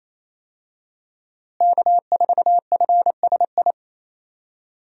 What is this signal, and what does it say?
Morse code, heard as a keyed tone, about 28 wpm, 700 Hz. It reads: K4FSI
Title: USA Call Signs - 5 Characters - Repeated 2x - 28wpm